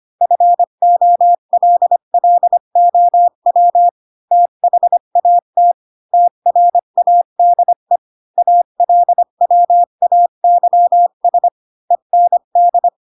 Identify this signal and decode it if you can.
Morse code, heard as a keyed tone, about 25 wpm, 700 Hz. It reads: FOLLOW THAT TRADE ALWAYS END
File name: Pinocchio / Pinocchio ch4 20 25 ik1hge